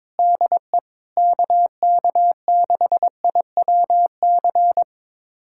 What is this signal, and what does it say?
Morse code, heard as a keyed tone, about 22 wpm, 700 Hz. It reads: DE KK6IWC
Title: USA Callsign DE Callsign - 6 Characters - 22wpm